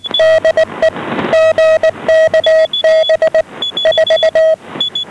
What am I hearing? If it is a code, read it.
Morse code, heard as a keyed tone, about 19 wpm, 640 Hz. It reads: DE GKB 4
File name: GKB4 12835.4 1130